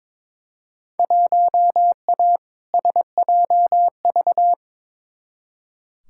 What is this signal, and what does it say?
Morse code, heard as a keyed tone, about 22 wpm, 700 Hz. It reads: 1A SJV
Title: Field Day Exchanges Encore - 22wpm